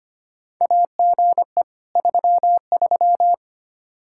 Morse code, heard as a keyed tone, about 25 wpm, 700 Hz. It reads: AGE 33
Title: K1USN SST QSO Practice 25wpm